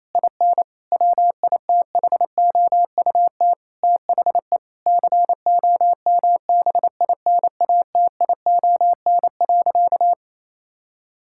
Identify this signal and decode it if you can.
Morse code, heard as a keyed tone, about 27 wpm, 700 Hz. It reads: IN WITHOUT THE COMBINATION.
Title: Sentences from Top 1000 Words - Review Part 6 - 28wpm